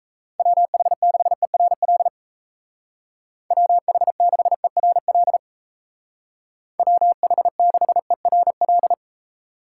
Morse code, heard as a keyed tone, about 35 wpm, 700 Hz. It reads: WH6ERL WH6ERL WH6ERL